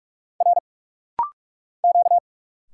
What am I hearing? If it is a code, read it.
Morse code, heard as a keyed tone, about 46 wpm, 700 Hz. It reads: R Q
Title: Single Letters - Rapid-Fire 45wpm